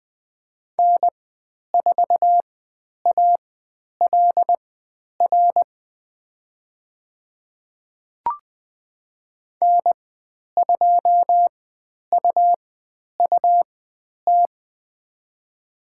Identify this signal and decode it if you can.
Morse code, heard as a keyed tone, about 20 wpm, spaced as for 10 wpm, 700 Hz. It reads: N4ALR N2UUT
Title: 102 Lesson 17g - Letter u - Call Signs 20wpm